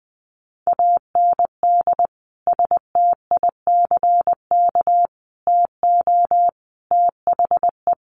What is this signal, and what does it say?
Morse code, heard as a keyed tone, about 20 wpm, 700 Hz. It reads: AND STICK TO THE
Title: Sentences from Top 1200 Words 20wpm